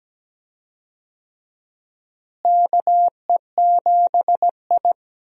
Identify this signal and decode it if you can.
Morse code, heard as a keyed tone, about 17 wpm, 700 Hz. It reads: KE7I
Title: USA Call Signs - 4 Characters - Repeated 2x - 17wpm